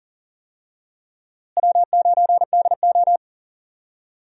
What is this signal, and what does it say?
Morse code, heard as a keyed tone, about 40 wpm, 700 Hz. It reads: W9DO
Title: USA Call Signs - 4 Characters - Repeated 3x - 40wpm